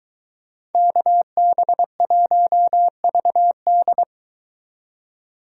Morse code, heard as a keyed tone, about 22 wpm, 700 Hz. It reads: KB1VD